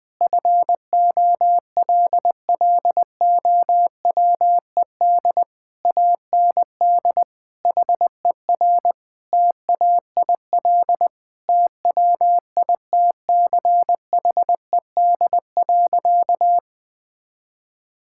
Morse code, heard as a keyed tone, about 20 wpm, 700 Hz. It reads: FOLLOWED AND HER TAIL TWITCHED.
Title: Sentences with 2-Word Phrases - Spoken Beginning - 20wpm